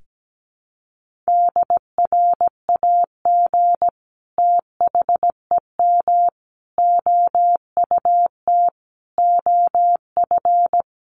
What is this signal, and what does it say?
Morse code, heard as a keyed tone, about 17 wpm, 700 Hz. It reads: DRAG THEM OUT OF
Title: Sentences from Top 1500 Words 17wpm